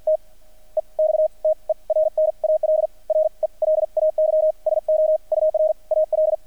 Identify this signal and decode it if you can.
Morse code, heard as a keyed tone, about 50 wpm, spaced as for 39 wpm, 620 Hz. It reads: T EXTERNAL RELAY SO UR AL
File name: archived / CRSnet